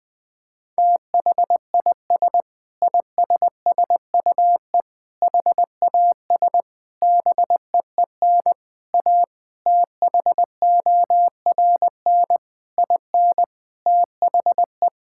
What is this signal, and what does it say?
Morse code, heard as a keyed tone, about 21 wpm, 700 Hz. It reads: THIS ISSUE HAS BEEN A THORN IN THE
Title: Sentences with 8-Word Phrases - 20wpm